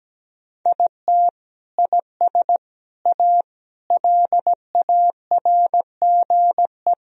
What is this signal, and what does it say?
Morse code, heard as a keyed tone, about 17 wpm, 700 Hz. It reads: IT IS A LARGE
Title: Sentences from Top 1000 Words - Review Part 4 - 17wpm